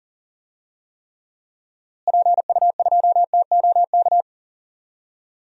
Morse code, heard as a keyed tone, about 40 wpm, 700 Hz. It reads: PU2TOK
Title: DXCC-100 Call Signs - 6 Characters - Repeated 3x - 40wpm